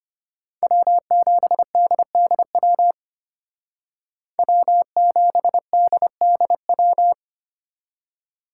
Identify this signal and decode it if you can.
Morse code, heard as a keyed tone, about 32 wpm, 700 Hz. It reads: W7DDW W7DDW